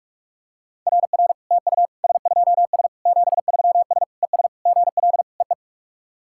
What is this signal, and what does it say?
Morse code, heard as a keyed tone, about 44 wpm, 700 Hz. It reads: RR TU S2S 73S ES GL EE
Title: SOTA QSO Practice 45wpm